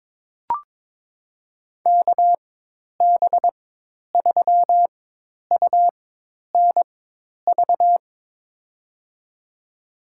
Morse code, heard as a keyed tone, about 21 wpm, spaced as for 10 wpm, 700 Hz. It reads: KB3UNV
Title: USA Call Signs - 6 Characters - 4x Character Spacing - 22wpm